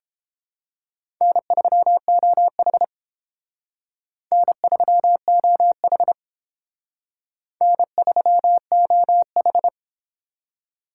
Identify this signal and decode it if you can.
Morse code, heard as a keyed tone, about 30 wpm, 700 Hz. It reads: N3OH N3OH N3OH